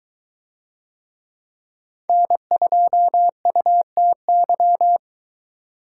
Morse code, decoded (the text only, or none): N2UTY